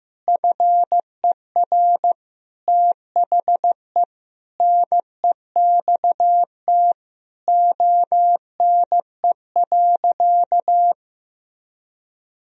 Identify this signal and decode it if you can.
Morse code, heard as a keyed tone, about 15 wpm, 700 Hz. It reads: FER THE NEXT ONE.
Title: Sentences from Top 400 Words and Abbreviations 15wpm